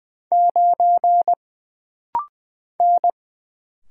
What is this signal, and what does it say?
Morse code, heard as a keyed tone, about 20 wpm, 700 Hz. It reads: 9 N
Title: Single Letter-Number Alt - Rapid-Fire 20wpm